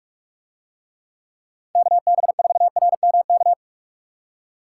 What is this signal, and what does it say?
Morse code, heard as a keyed tone, about 44 wpm, 700 Hz. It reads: KB4RMK